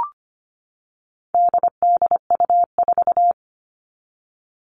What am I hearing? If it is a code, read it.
Morse code, heard as a keyed tone, about 25 wpm, 700 Hz. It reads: DDU4
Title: All Cut Numbers - 4-Digit 25wpm